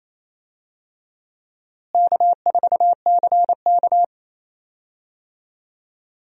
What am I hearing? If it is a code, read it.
Morse code, heard as a keyed tone, about 28 wpm, 700 Hz. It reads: K4CK
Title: USA Call Signs - 4 Characters - Repeated 2x - 28wpm